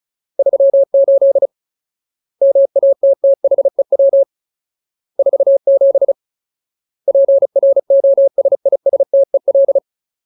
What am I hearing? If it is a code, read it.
Morse code, heard as a keyed tone, about 34 wpm, 550 Hz. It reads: 38 MATTHEW 47 PROSISTEL
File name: Ultimate Word List / 35WPM Ultimate